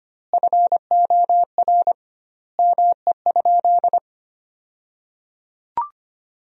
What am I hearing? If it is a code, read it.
Morse code, heard as a keyed tone, about 25 wpm, 700 Hz. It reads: FOR ME?
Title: 2x Word-Spacing - Sentences from Top 200 Words - 25wpm